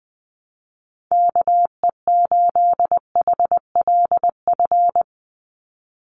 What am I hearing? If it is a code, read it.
Morse code, heard as a keyed tone, about 20 wpm, 700 Hz. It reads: KE8HLF
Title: USA Call Signs 20wpm